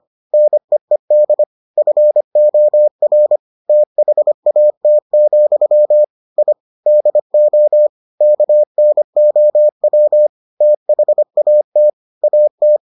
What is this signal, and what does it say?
Morse code, heard as a keyed tone, about 25 wpm, 600 Hz. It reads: NEED FOR THAT, I DO KNOW THAT AT